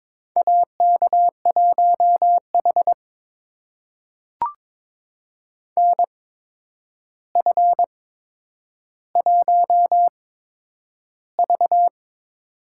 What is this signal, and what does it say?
Morse code, heard as a keyed tone, about 22 wpm, spaced as for 9 wpm, 700 Hz. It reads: AK1H NF1V
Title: USA Call Signs - 4 Characters - 8x Character Spacing - 22wpm